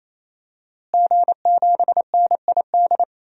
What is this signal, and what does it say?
Morse code, heard as a keyed tone, about 27 wpm, 700 Hz. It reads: G7NID
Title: DXCC-100 Call Signs - 5 Characters - Repeated 3x - 28wpm